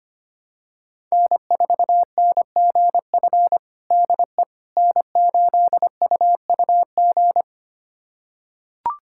Morse code, heard as a keyed tone, about 25 wpm, 700 Hz. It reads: N4NGF DE N8UUG
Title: USA Callsign DE Callsign - 5 Characters - 25wpm